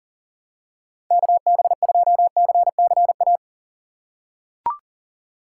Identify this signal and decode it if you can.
Morse code, heard as a keyed tone, about 40 wpm, 700 Hz. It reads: KB2CCA
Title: USA Call Signs - 6 Characters - 4x Character Spacing - 40wpm